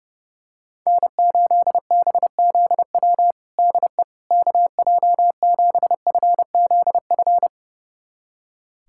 Morse code, heard as a keyed tone, about 30 wpm, 700 Hz. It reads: N8BZW DE KJ7FZF
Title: USA Callsign DE Callsign 30wpm